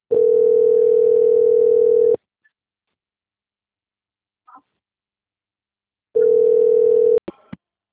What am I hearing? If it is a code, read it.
Ringback tone (North America), 440 + 480 Hz.
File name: out- - - - - .11522